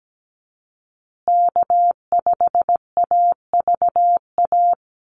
Morse code, heard as a keyed tone, about 17 wpm, 700 Hz. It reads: K5AVA
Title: USA Call Signs Repeated 3x 17wpm